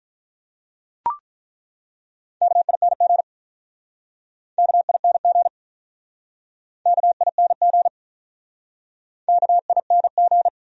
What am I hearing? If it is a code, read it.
Morse code, heard as a keyed tone, about 48 wpm, 700 Hz. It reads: KING KING KING KING